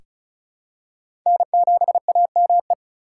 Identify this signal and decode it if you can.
Morse code, heard as a keyed tone, about 34 wpm, 700 Hz. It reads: N7AME
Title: K1USN SST QSO Practice - 35wpm